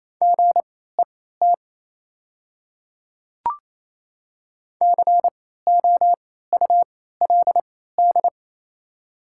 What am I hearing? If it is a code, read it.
Morse code, heard as a keyed tone, about 28 wpm, spaced as for 16 wpm, 700 Hz. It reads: GET COULD
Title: 3x Character Spacing - Top 500 Words - 28wpm